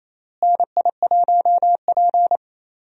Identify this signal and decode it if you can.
Morse code, heard as a keyed tone, about 28 wpm, 700 Hz. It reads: NI1P
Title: USA Call Signs - 4 Characters - Repeated 3x - 28wpm